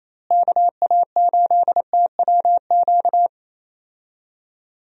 Morse code, heard as a keyed tone, about 27 wpm, 700 Hz. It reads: KA8TWQ
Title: USA Call Signs - 6 Characters - Repeated 3x - 28wpm